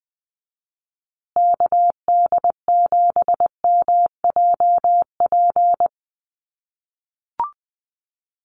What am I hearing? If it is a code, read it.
Morse code, heard as a keyed tone, about 20 wpm, 700 Hz. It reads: KD7MJP
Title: USA Call Signs Repeated 3x 20wpm